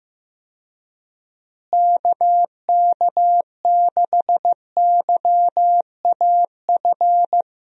Morse code, heard as a keyed tone, about 15 wpm, 700 Hz. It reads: KK6YAF
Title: USA Call Signs - 6 Characters - Repeated 3x - 15wpm